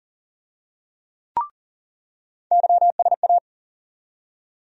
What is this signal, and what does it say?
Morse code, heard as a keyed tone, about 40 wpm, 700 Hz. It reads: YSA
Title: USA Call Sign Postfixes - 3 Letters 40wpm